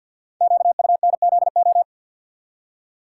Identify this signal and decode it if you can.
Morse code, heard as a keyed tone, about 50 wpm, 690 Hz. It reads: QUNZO
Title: Code Groups - 5 Letters - 50wpm